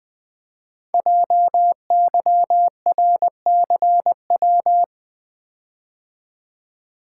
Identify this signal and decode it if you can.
Morse code, heard as a keyed tone, about 20 wpm, 700 Hz. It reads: JYRCW
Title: Code Groups - 5 Letter:Numbers - 20wpm